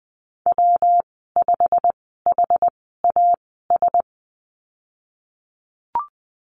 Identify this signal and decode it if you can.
Morse code, heard as a keyed tone, about 20 wpm, spaced as for 14 wpm, 700 Hz. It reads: W5HAS
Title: 122 Lesson 20g - Number 3 - Call Signs 20wpm